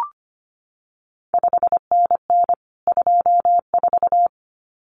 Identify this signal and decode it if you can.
Morse code, heard as a keyed tone, about 25 wpm, 700 Hz. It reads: 5NN 24
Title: CQ WW DX Contest Exchange Encore - 25wpm